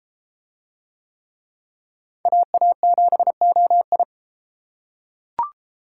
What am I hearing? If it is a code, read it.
Morse code, heard as a keyed tone, about 32 wpm, 700 Hz. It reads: AA7OI